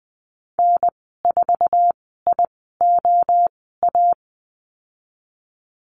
Morse code, heard as a keyed tone, about 20 wpm, spaced as for 15 wpm, 700 Hz. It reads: N4IOA
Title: 048 Lesson 9g - Number 4 - Call Signs 20wpm